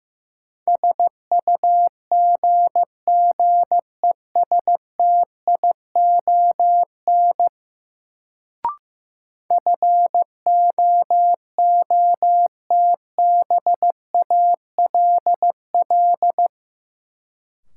Morse code, heard as a keyed tone, about 15 wpm, 700 Hz. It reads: SUGGESTION FOOTBALL
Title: Sentences from Top 1600 Words - New Word Drill Alt - 15wpm